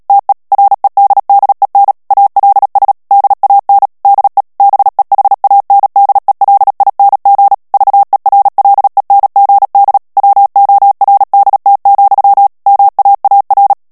Morse code, heard as a keyed tone, about 37 wpm, 800 Hz. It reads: TE REDDEN ALS DAN DE BEHANDELING VERLENGD WORDT, MAAR
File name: morse mp3 / HOLLANDAIS texte clair 2 vitesse 35 mots minute